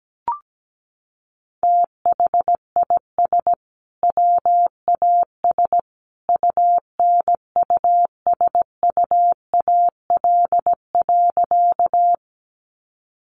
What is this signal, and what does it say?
Morse code, heard as a keyed tone, about 17 wpm, 700 Hz. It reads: THIS WAS UNUSUAL.
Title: Sentences from Top 1400 Words 17wpm